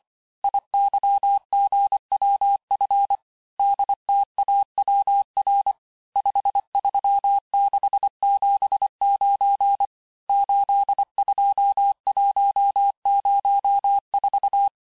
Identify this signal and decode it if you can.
Morse code, heard as a keyed tone, about 25 wpm, 800 Hz. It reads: IYGWF DTAWR 53679 82104
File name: texte-chiffre-vitesse-23mpm